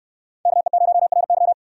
Morse code, heard as a keyed tone, about 69 wpm, 700 Hz. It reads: KE0RY